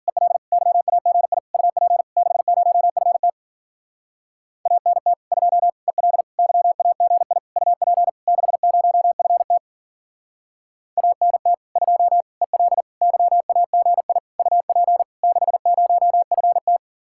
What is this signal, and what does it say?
Morse code, heard as a keyed tone, about 49 wpm, 700 Hz. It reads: EL YAGI UP 60FT ANT 2 EL YAGI UP 60FT ANT 2 EL YAGI UP 60FT
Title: Speed-Racing Common QSO Phrases - 40wpm